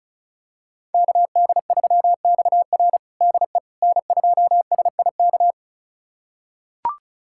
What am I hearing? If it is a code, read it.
Morse code, heard as a keyed tone, about 36 wpm, 700 Hz. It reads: KD3XR DE N2SIK
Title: USA Callsign DE Callsign - 5 Characters - 35wpm